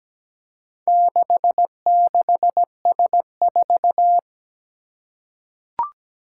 Morse code, heard as a keyed tone, about 17 wpm, 700 Hz. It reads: 66S4
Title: Four Challenging Letter-Number Combinations 17wpm